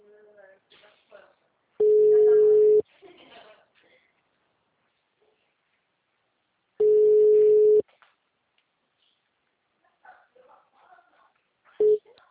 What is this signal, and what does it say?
Ringback tone (Europe, 425 Hz), with speech.